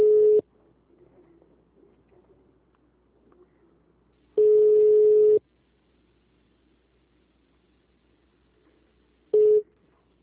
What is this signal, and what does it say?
Ringback tone (Europe, 425 Hz).
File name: out- -2006- - - .50